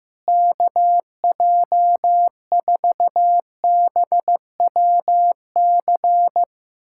Morse code, heard as a keyed tone, about 15 wpm, 700 Hz. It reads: KJ4BWC
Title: USA Callsign DE Callsign 15wpm